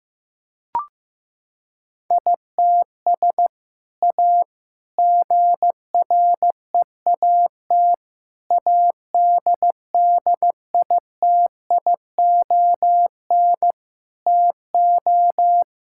Morse code, heard as a keyed tone, about 15 wpm, 700 Hz. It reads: ITS A GREAT ADDITION TO
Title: Sentences from Top 1000 Words - Review Part 6 - 15wpm